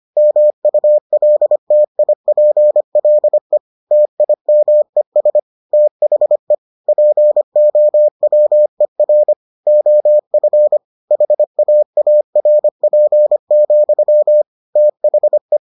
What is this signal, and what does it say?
Morse code, heard as a keyed tone, about 25 wpm, 600 Hz. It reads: MULTIPLE TIMES THE POWER OF HAARP, THE